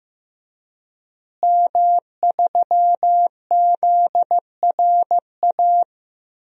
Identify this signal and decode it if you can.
Morse code, heard as a keyed tone, about 15 wpm, 700 Hz. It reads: M3ZRA
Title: DXCC-100 Call Signs - 5 Characters - Repeated 2x - 15wpm